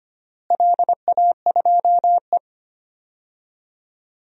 Morse code, heard as a keyed tone, about 25 wpm, 700 Hz. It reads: LA2E
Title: DXCC-100 Call Signs - 4 Characters - Repeated 3x - 25wpm